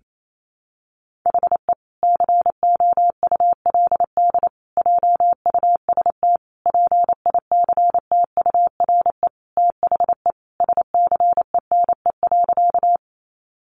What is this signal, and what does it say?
Morse code, heard as a keyed tone, about 28 wpm, 700 Hz. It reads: HE COULD JUST PICTURE THE SCENE.
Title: Sentences from Top 1100 Words 28wpm